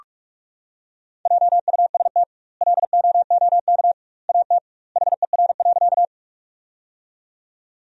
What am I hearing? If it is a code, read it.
Morse code, heard as a keyed tone, about 46 wpm, 700 Hz. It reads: JUST LOOK AT HER.
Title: 2x Word-Spacing - Sentences from Top 200 Words - 45wpm